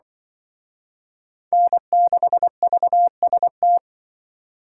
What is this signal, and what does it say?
Morse code, heard as a keyed tone, about 24 wpm, 700 Hz. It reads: N6VST
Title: Speed Racing - USA 1x3 Call Signs 20wpm